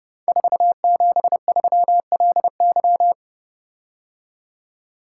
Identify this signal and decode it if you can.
Morse code, heard as a keyed tone, about 30 wpm, 700 Hz. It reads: 473LY
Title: Code Groups - 5 Letter:Numbers - 30wpm